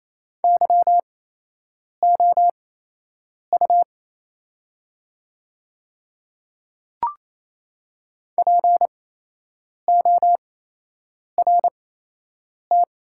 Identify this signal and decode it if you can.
Morse code, heard as a keyed tone, about 28 wpm, spaced as for 8 wpm, 700 Hz. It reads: YOU PORT
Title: 8x Character Spacing - Top 200 Words - 28wpm